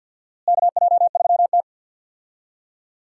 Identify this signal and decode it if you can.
Morse code, heard as a keyed tone, about 47 wpm, 700 Hz. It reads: KJ3T
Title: USA Call Signs - 4 Characters - Repeated 3x - 50wpm